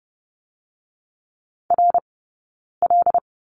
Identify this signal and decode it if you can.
Morse code, heard as a keyed tone, about 30 wpm, 700 Hz. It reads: R L
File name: course / 30:10wpm / 099 Lesson 17d - Letter u - Two Characters 30wpm